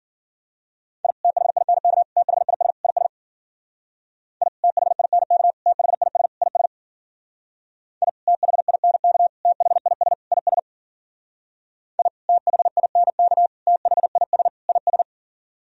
Morse code, heard as a keyed tone, about 54 wpm, 700 Hz. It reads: I THINK THIS IS I THINK THIS IS I THINK THIS IS I THINK THIS IS